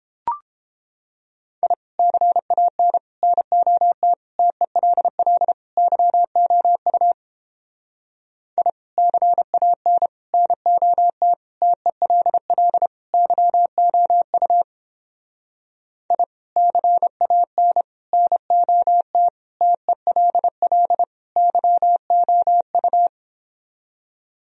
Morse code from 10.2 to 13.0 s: NOT TELL

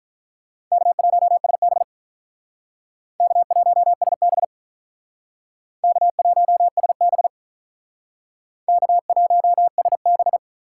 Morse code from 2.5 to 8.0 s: K1SB K1SB

